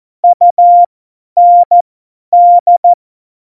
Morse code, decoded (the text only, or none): UND